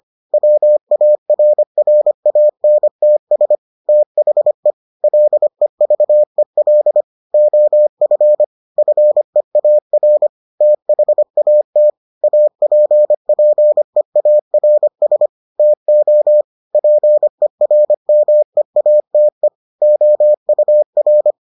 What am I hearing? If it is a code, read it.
Morse code, heard as a keyed tone, about 25 wpm, 600 Hz. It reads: WARRANTS THE LEVEL OF FEAR THAT APPEARS TO PERMEATE OUR